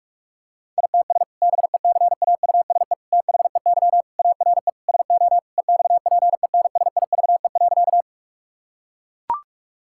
Morse code, heard as a keyed tone, about 44 wpm, 700 Hz. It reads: ITS BECAUSE THEY ARE SO EXPENSIVE.
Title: Sentences from Top 1000 Words - Review Part 6 - 45wpm